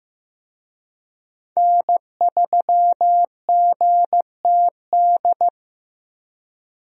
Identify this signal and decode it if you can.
Morse code, heard as a keyed tone, about 15 wpm, 700 Hz. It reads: N3GTD
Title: USA Call Signs - 5 Characters - Repeated 2x - 15wpm